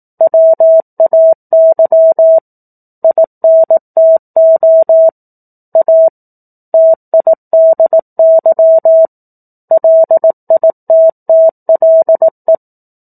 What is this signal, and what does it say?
Morse code, heard as a keyed tone, about 18 wpm, 650 Hz. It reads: WAY INTO A TIDY LITTLE
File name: 18 WPM